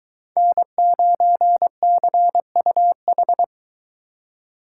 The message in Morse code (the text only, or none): N9CUH